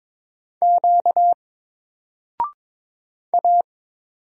Morse code, heard as a keyed tone, about 22 wpm, 700 Hz. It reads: Q A
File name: Single-Letter-Number-22wpm